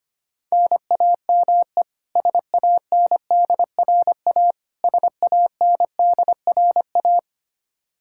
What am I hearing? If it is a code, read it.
Morse code, heard as a keyed tone, about 25 wpm, 700 Hz. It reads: NAME SANDRA SANDRA
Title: Practice QSO - Exchange 1 - Beginning - 25wpm